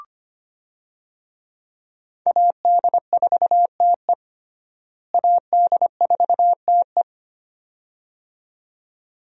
Morse code, heard as a keyed tone, about 25 wpm, 700 Hz. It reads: AD4TE AD4TE
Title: USA Call Signs - 5 Characters - Repeated 2x - 25wpm